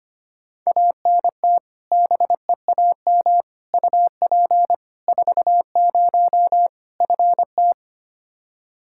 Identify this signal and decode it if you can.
Morse code, heard as a keyed tone, about 25 wpm, 700 Hz. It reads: ANT BEAM UP 40 FT
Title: Practice QSO - Exchange 2 - Info - 25wpm